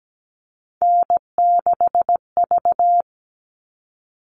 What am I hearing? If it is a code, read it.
Morse code, heard as a keyed tone, about 17 wpm, 700 Hz. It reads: N6V
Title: All Cut Numbers - 3-Digit 17wpm